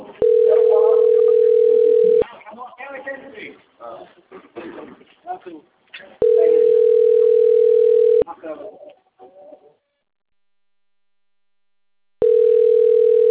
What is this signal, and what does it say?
Ringback tone (North America), 440 + 480 Hz, with speech.
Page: Itatiba 2